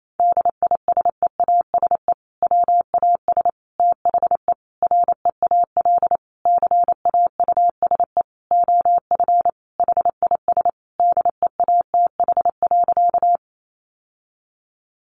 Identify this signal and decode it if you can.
Morse code, heard as a keyed tone, about 28 wpm, 700 Hz. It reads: DISEASE WAS THE REAL CAUSE OF HIS DEATH.
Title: Sentences from Top 1100 Words 28wpm